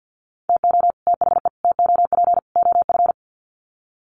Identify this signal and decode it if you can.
Morse code, heard as a keyed tone, about 48 wpm, 700 Hz. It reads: TO THE TOP OF
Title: Sets of 4 Words 50wpm